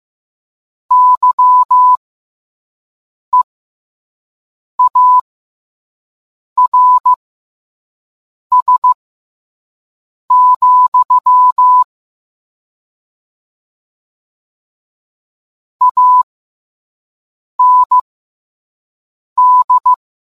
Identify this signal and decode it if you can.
Morse code, heard as a keyed tone, about 15 wpm, spaced as for 5 wpm, 1000 Hz. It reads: YEARS, AND